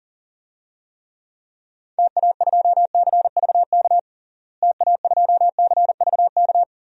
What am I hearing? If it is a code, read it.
Morse code, heard as a keyed tone, about 37 wpm, 700 Hz. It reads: TA2CVK TA2CVK
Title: DXCC-100 Call Signs - 6 Characters - Repeated 3x - 40wpm